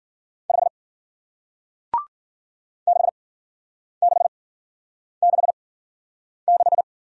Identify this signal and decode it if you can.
Morse code, heard as a keyed tone, about 55 wpm, 700 Hz. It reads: 5 6 6 6 6